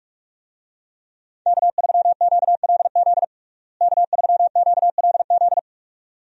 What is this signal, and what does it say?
Morse code, heard as a keyed tone, about 46 wpm, 700 Hz. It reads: K3QLZ K3QLZ
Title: USA Call Signs - 5 Characters - Repeated 2x - 45wpm